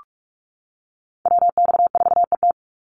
Morse code, heard as a keyed tone, about 44 wpm, 700 Hz. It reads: WX4ET
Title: NA SOTA Chaser Callsigns 45wpm